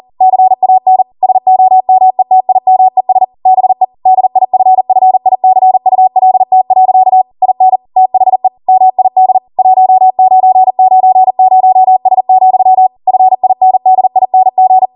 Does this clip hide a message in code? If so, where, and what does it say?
Yes: Morse code from 0.2 to 14.9 s, heard as a keyed tone, about 40 wpm, 750 Hz. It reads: CAN SOMETIMES BE DIFFICULT. IN THE MID 1990S, FINDING